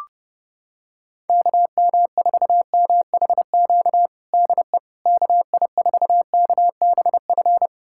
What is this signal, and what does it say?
Morse code, heard as a keyed tone, about 30 wpm, 700 Hz. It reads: KM4MHQ DE KI4KBF
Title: USA Callsign DE Callsign - 6 Characters - 30wpm